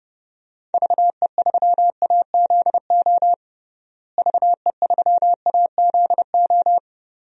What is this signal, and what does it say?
Morse code, heard as a keyed tone, about 30 wpm, 700 Hz. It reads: VE3AZO VE3AZO
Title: DXCC-100 Call Signs - 6 Characters - Repeated 2x - 30wpm